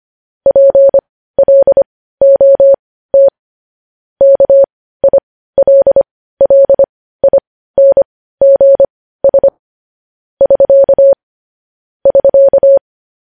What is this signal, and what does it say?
Morse code, heard as a keyed tone, about 25 wpm, spaced as for 15 wpm, 550 Hz. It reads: PLOT KILLINGS <SK> <SK>